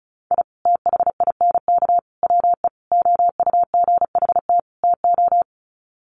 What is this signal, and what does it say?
Morse code, heard as a keyed tone, about 34 wpm, 700 Hz. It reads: I THINK WE OUGHT TO
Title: Sets of 5 Words Encore 35wpm